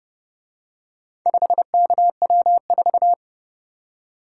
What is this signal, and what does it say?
Morse code, heard as a keyed tone, about 29 wpm, 700 Hz. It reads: 5KW4